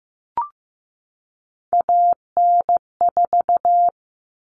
Morse code, heard as a keyed tone, about 15 wpm, 700 Hz. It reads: AN4
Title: All Cut Numbers - 3-Digit 15wpm